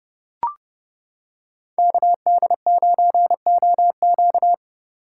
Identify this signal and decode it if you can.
Morse code, heard as a keyed tone, about 29 wpm, 700 Hz. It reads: KD9OQ